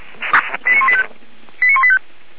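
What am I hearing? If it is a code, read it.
Five-tone selective call (ZVEI), heard as quick runs of five tones, tones of about 70 ms. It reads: 89185 89185
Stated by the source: ZVEI five-tone selective call received on an SDR.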